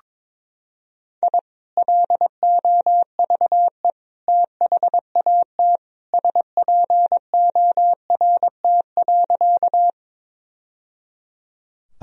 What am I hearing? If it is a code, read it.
Morse code, heard as a keyed tone, about 22 wpm, 700 Hz. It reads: I LOVE THAT SPORT.
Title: Sentences from Top 1000 Words - Review Part 4 - 22wpm